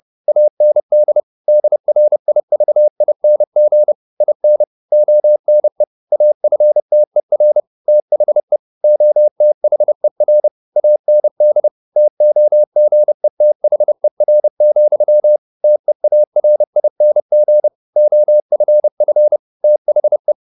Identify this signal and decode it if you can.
Morse code, heard as a keyed tone, about 30 wpm, 600 Hz. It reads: AND DRIVING IN ONE AFTER THE OTHER AND TOGETHER, TEARING OFF THE